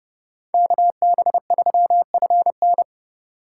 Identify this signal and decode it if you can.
Morse code, heard as a keyed tone, about 30 wpm, 700 Hz. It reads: KB3FN